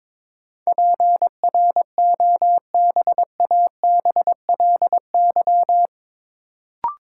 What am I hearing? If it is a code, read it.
Morse code, heard as a keyed tone, about 22 wpm, 700 Hz. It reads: PROBABLY
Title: Sentences from Top 700 Words - New Word Drill - 22wpm